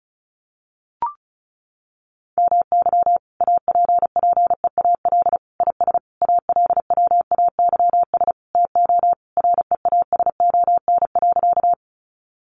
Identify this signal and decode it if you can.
Morse code, heard as a keyed tone, about 34 wpm, 700 Hz. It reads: MY APPEAL IS ALWAYS TO REASON.
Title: Sentences from Top 1300 Words 35wpm